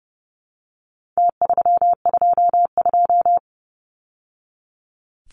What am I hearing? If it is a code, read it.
Morse code, heard as a keyed tone, about 29 wpm, 700 Hz. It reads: T322
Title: Numbers - Contest Style - 4-Digit 30wpm